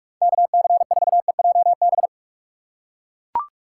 Morse code, heard as a keyed tone, about 43 wpm, 700 Hz. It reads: KC4EJB
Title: USA Call Signs - 6 Characters - 4x Character Spacing - 45wpm